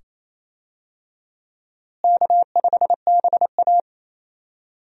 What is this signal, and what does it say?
Morse code, heard as a keyed tone, about 28 wpm, 700 Hz. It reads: K5BA
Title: USA Call Signs - 4 Characters - Repeated 2x - 28wpm